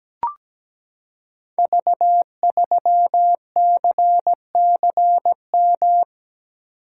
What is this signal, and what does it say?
Morse code, heard as a keyed tone, about 17 wpm, 700 Hz. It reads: V3CCM
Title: Code Groups - 5 Letter:Numbers - 17wpm